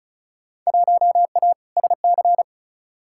Morse code, heard as a keyed tone, about 36 wpm, 700 Hz. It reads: 1A SC
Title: Field Day Exchanges - 35wpm